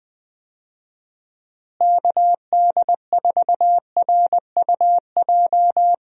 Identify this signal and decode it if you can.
Morse code, heard as a keyed tone, about 20 wpm, 700 Hz. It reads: KD4RUJ
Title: USA Call Signs - 6 Characters - Repeated 2x - 20wpm